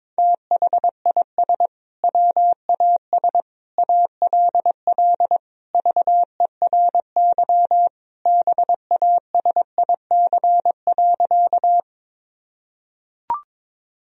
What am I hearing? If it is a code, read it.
Morse code, heard as a keyed tone, about 22 wpm, 690 Hz. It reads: THIS WAS ALL VERY BASIC.
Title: Sentences from Top 1000 Words - Review Part 3 - 22wpm